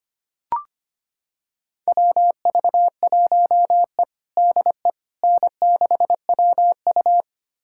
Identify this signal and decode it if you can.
Morse code, heard as a keyed tone, about 25 wpm, 700 Hz. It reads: WV1E DE N6WU
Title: USA Callsign DE Callsign - 4 Characters - 25wpm